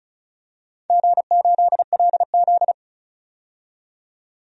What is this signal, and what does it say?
Morse code, heard as a keyed tone, about 34 wpm, 700 Hz. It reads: G8LZ
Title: DXCC-100 Call Signs - 4 Characters - 35wpm